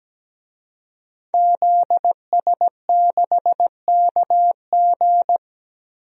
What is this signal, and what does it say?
Morse code, heard as a keyed tone, about 17 wpm, 700 Hz. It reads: ZS6KG
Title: DXCC-100 Call Signs - 5 Characters - 17wpm